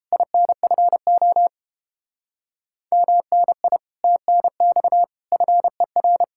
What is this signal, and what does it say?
Morse code, heard as a keyed tone, about 29 wpm, 700 Hz. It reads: INFO MNI TNX FER